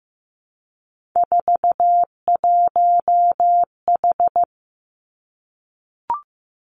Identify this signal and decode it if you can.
Morse code, heard as a keyed tone, about 15 wpm, 700 Hz. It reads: 41H
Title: Three Challenging Letter-Number Combinations 15wpm